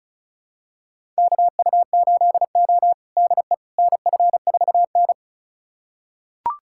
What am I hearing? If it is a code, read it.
Morse code, heard as a keyed tone, about 36 wpm, 700 Hz. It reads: KU8O DE NF4N
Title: USA Callsign DE Callsign - 4 Characters - 35wpm